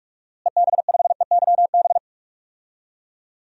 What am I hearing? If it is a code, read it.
Morse code, heard as a keyed tone, about 46 wpm, 700 Hz. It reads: EB5EYB
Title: DXCC-100 Call Signs - 6 Characters - Repeated 2x - 45wpm